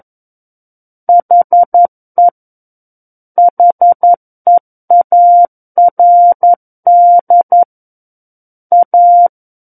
Morse code, heard as a keyed tone, about 11 wpm, 700 Hz. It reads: HE HEARD A
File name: Book One / 02-Chapter 2 - 11 WPM